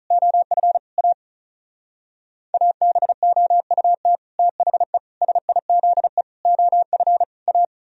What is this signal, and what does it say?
Morse code, heard as a keyed tone, about 42 wpm, 700 Hz. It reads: OF A ABOUT THE SIZE OF A
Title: Speed-Racing - Sets of 5 Words 35wpm